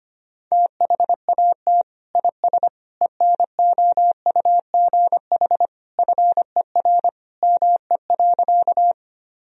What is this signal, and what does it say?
Morse code, heard as a keyed tone, about 25 wpm, 700 Hz. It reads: THAT IS ENOUGH FER ME.
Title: Sentences from Top 400 Words and Abbreviations 25wpm